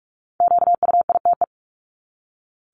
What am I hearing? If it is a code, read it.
Morse code, heard as a keyed tone, about 44 wpm, 700 Hz. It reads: QUITE
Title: Top 501-750 Words 45wpm